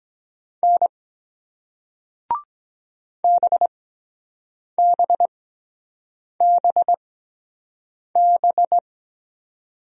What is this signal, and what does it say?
Morse code, heard as a keyed tone, about 23 wpm, spaced as for 7 wpm, 700 Hz. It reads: N BBBB